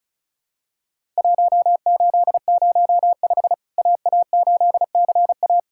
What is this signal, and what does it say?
Morse code, heard as a keyed tone, about 36 wpm, 700 Hz. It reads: 1805 AA8CA
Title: Worked All Europe DX Contest - QTC Exchange - 35wpm